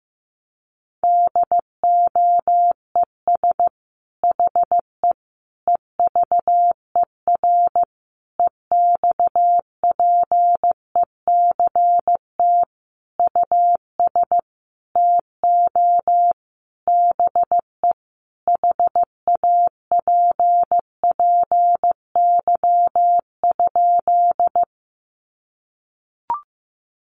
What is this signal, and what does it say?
Morse code, heard as a keyed tone, about 15 wpm, 700 Hz. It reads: DOES HE EVER EXPECT US TO BE HAPPY?
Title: Sentences from Top 1100 Words 15wpm